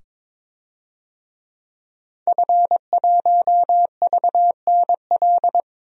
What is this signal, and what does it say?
Morse code, heard as a keyed tone, about 22 wpm, 700 Hz. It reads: F1VNL
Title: DXCC-100 Call Signs - 5 Characters - 22wpm